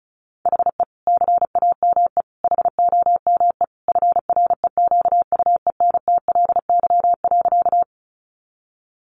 Morse code, heard as a keyed tone, about 34 wpm, 700 Hz. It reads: HE CAME HOME FREQUENTLY.
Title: Sentences from Top 1300 Words 35wpm